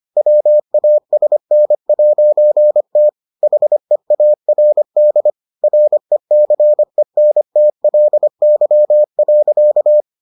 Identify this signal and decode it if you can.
Morse code, heard as a keyed tone, about 25 wpm, 600 Hz. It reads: WASN'T HEARD RECENTLY.